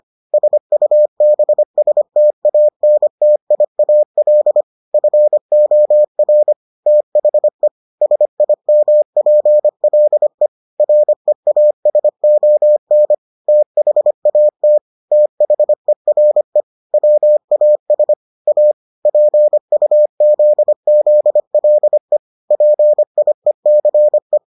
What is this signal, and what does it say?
Morse code, heard as a keyed tone, about 25 wpm, 600 Hz. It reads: SUBSTANTIAL FOR THE SIMPLE REASON THAT THERE WAS A PUZZLE PIECE